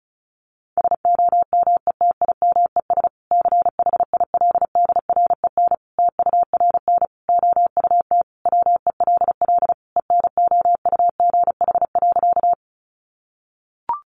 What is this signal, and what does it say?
Morse code, heard as a keyed tone, about 34 wpm, 700 Hz. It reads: SOMETIMES CHILDREN TURN OUT WELL ENOUGH.
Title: Sentences from Top 600 Words 35wpm